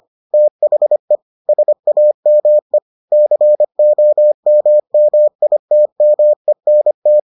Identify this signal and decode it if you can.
Morse code, heard as a keyed tone, about 25 wpm, 600 Hz. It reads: THE SAME COMMITMENT